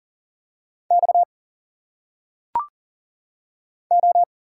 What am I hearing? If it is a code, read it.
Morse code, heard as a keyed tone, about 40 wpm, 700 Hz. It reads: X O
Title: Single Letters - Phonetic Pronunciation 40wpm